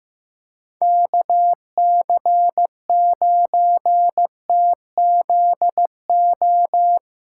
Morse code, heard as a keyed tone, about 15 wpm, 700 Hz. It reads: KC9TZO
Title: K1USN SST QSO Practice - 15wpm